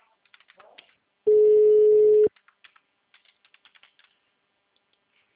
Ringback tone (Europe, 425 Hz).
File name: out- -2006- - - .4402